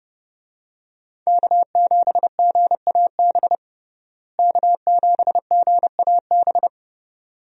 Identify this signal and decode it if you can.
Morse code, heard as a keyed tone, about 32 wpm, 700 Hz. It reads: K7GAB K7GAB
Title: USA Call Signs - 5 Characters - Repeated 2x - 30wpm